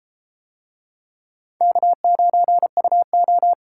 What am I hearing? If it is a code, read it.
Morse code, heard as a keyed tone, about 33 wpm, 700 Hz. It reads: K9UO